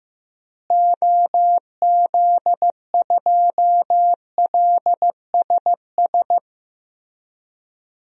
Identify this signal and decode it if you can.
Morse code, heard as a keyed tone, about 15 wpm, 700 Hz. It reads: OZ2LSS
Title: DXCC-100 Call Signs - 6 Characters - Repeated 3x - 15wpm